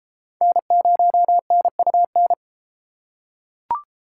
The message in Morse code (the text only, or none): N0NUN